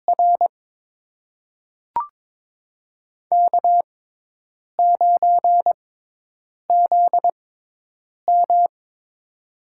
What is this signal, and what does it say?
Morse code, heard as a keyed tone, about 22 wpm, spaced as for 8 wpm, 700 Hz. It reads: R K9ZM
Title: USA Call Signs - 4 Characters - 6x Character Spacing - 22wpm